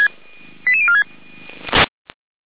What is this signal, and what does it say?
Five-tone selective call (ZVEI), heard as quick runs of five tones, tones of about 80 ms. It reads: 88046